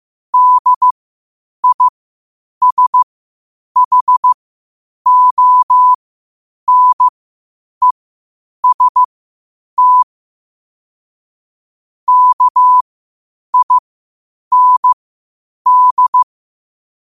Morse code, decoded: DISHONEST KIND

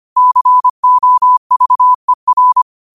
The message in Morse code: COVER

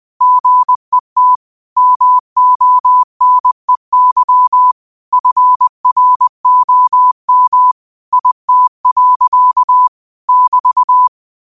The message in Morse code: GET MONEY FROM IT. <BT>

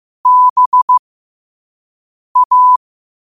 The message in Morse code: BA

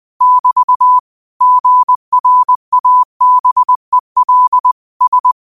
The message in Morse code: <BT> GRABEL S

